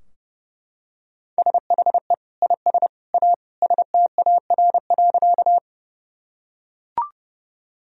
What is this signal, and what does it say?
Morse code, heard as a keyed tone, about 32 wpm, 700 Hz. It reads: SHE IS A STAR.
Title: Sentences from Top 1000 Words - Review Part 2 - 30wpm